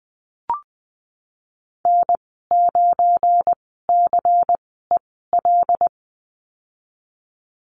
Morse code, heard as a keyed tone, about 20 wpm, spaced as for 15 wpm, 700 Hz. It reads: N9CEL
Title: 179 Lesson 29g - Slash - Call Signs 20wpm